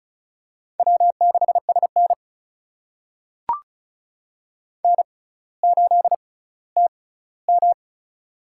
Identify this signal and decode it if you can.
Morse code, heard as a keyed tone, about 36 wpm, spaced as for 18 wpm, 700 Hz. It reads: W6SN N8TM
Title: USA Call Signs - 6x Character Spacing - 35wpm